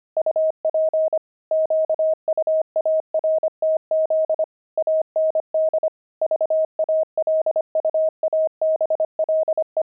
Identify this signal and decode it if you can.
Morse code, heard as a keyed tone, about 25 wpm, 630 Hz. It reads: UP QUARTZ AND VALUABLE
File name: pangram-fs-25-cs-25-grandpa